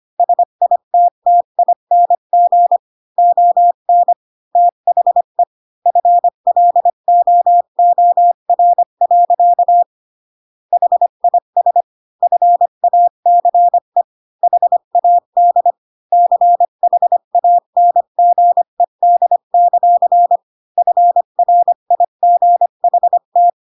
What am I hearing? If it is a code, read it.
Morse code, heard as a keyed tone, about 25 wpm, 700 Hz. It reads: SITTING ON THE FLOOR. HIS FACE HAD CHANGED; FRIGHT
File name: Pinocchio / Pinocchio ch1 20 25 ik1hge